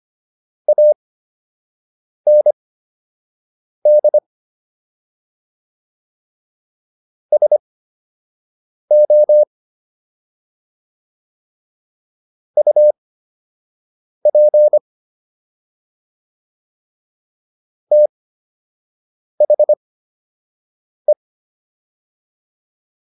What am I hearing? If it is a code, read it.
Morse code, heard as a keyed tone, about 25 wpm, spaced as for 6 wpm, 600 Hz. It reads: AND SO UP THE